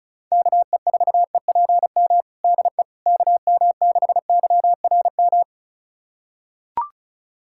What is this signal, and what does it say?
Morse code, heard as a keyed tone, about 34 wpm, 700 Hz. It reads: KE4EPM DE KM6YRM
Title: USA Callsign DE Callsign - 6 Characters - 35wpm